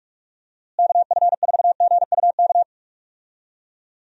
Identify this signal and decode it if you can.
Morse code, heard as a keyed tone, about 44 wpm, 700 Hz. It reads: KF4GUK
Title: USA Call Signs - 6 Characters - Repeated 2x - 45wpm